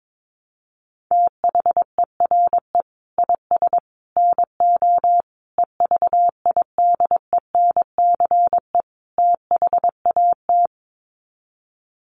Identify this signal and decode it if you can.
Morse code, heard as a keyed tone, about 22 wpm, 700 Hz. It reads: THERE IS NO EVIDENCE THAT
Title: Sets of 5 Words Encore 22wpm